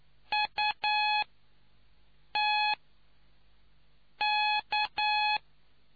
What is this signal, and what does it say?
Morse code, heard as a keyed tone, about 9 wpm, spaced as for 5 wpm, 840 Hz. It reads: UTK